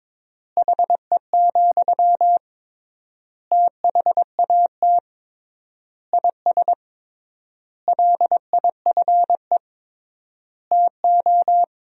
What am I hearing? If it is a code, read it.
Morse code, heard as a keyed tone, about 22 wpm, 700 Hz. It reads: HE, THAT IS LIFE TO
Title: 3x Word-Spacing - Sentences from Top 500 Words - 22wpm